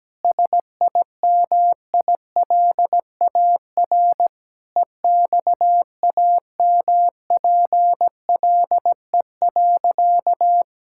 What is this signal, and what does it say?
Morse code, heard as a keyed tone, about 17 wpm, 700 Hz. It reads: SIMILAR EXAMPLE.
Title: Sentences from Top 1000 Words - Review Part 5 - 17wpm